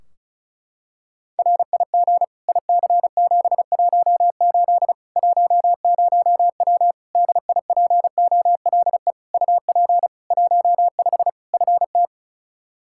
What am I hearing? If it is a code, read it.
Morse code, heard as a keyed tone, about 36 wpm, 700 Hz. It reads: RIG IC718 10W DIPOLE UP 15 FT
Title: Practice QSO - Exchange 2 - Info - 35wpm